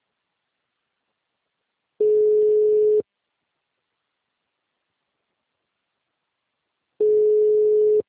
Ringback tone (Europe, 425 Hz).